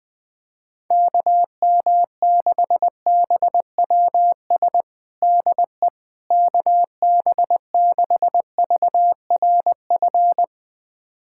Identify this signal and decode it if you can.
Morse code, heard as a keyed tone, about 20 wpm, 700 Hz. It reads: KM6BWS DE KB6VRF
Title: USA Callsign DE Callsign - 6 Characters - 20wpm